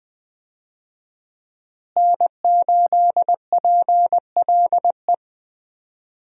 Morse code, heard as a keyed tone, about 20 wpm, 700 Hz. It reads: N8PLE
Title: USA Call Signs - 5 Characters - Repeated 3x - 20wpm